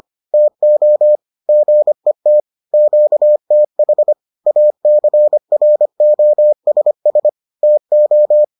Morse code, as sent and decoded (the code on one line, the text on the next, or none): - --- / --. - / --.- - .... / .- -.-. .-. --- ... ... / - ---
TO GET QTH ACROSS TO